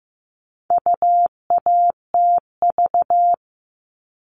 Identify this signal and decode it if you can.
Morse code, heard as a keyed tone, about 15 wpm, 700 Hz. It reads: UATV
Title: All Cut Numbers - 4-Digit 15wpm